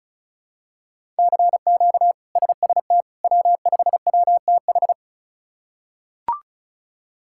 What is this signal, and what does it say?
Morse code, heard as a keyed tone, about 36 wpm, 700 Hz. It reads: CQ SST W5WTH
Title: K1USN SST QSO Practice - 35wpm